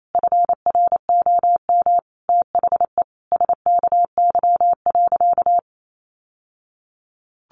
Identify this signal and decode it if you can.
Morse code, heard as a keyed tone, about 28 wpm, 700 Hz. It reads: FROM THE SKY.
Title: Sentences from Top 1200 Words 28wpm